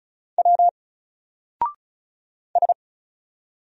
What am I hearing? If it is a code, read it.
Morse code, heard as a keyed tone, about 34 wpm, 700 Hz. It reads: W S
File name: Single-Letter-Number-35wpm